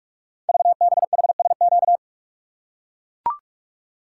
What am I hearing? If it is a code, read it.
Morse code, heard as a keyed tone, about 44 wpm, 700 Hz. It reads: VBHSQ
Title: Code Groups - 5 Letter:Numbers - 45wpm